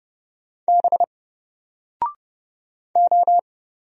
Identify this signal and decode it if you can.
Morse code, heard as a keyed tone, about 30 wpm, 700 Hz. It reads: B O
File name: Single-Letter-Number-30wpm